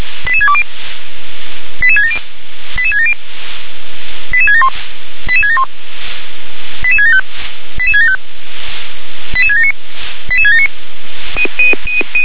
Five-tone selective call (ZVEI), heard as quick runs of five tones, tones of about 70 ms. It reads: 88620 88670 88670 88671 88671 88675 88675 88679 88679
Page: SDR/ZVEI